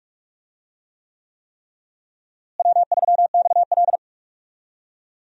Morse code, heard as a keyed tone, about 46 wpm, 700 Hz. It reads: W3XL
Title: USA Call Signs - 4 Characters - Repeated 3x - 45wpm